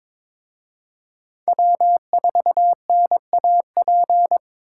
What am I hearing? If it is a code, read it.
Morse code, heard as a keyed tone, about 26 wpm, 700 Hz. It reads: W4NAP
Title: Speed Racing - USA 1x3 Call Signs 22wpm